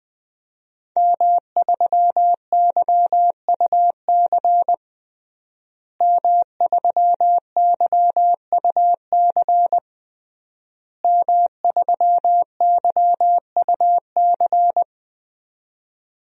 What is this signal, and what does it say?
Morse code, heard as a keyed tone, about 20 wpm, 700 Hz. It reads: M3YUC M3YUC M3YUC
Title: DXCC-100 Call Signs - 5 Characters - Repeated 3x - 20wpm